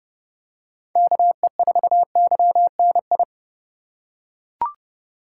Morse code, heard as a keyed tone, about 28 wpm, 700 Hz. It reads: KE4YNI